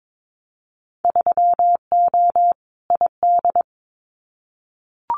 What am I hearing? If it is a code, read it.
Morse code, heard as a keyed tone, about 22 wpm, 700 Hz. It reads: 3O ID
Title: Winter Field Day Exchanges Encore 22wpm